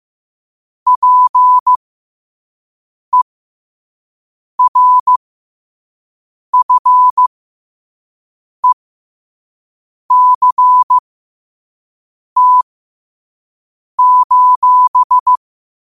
Morse code, heard as a keyed tone, about 15 wpm, spaced as for 5 wpm, 1000 Hz. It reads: PERFECT:
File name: qotdfiles / 5wpm